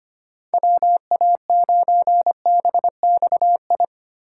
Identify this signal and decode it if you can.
Morse code, heard as a keyed tone, about 25 wpm, 700 Hz. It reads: WA9BXI
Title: USA Call Signs - 6 Characters - 2x Character Spacing - 25wpm